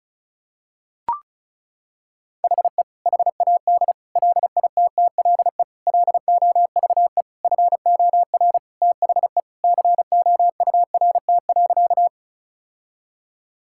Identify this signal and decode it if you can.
Morse code, heard as a keyed tone, about 36 wpm, 700 Hz. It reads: HE HAD LITTLE LOVE FOR THE COURT.
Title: Sentences from Top 1600 Words 35wpm